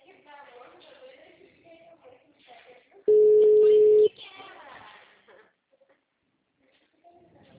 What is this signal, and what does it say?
Ringback tone (Europe, 425 Hz), with speech.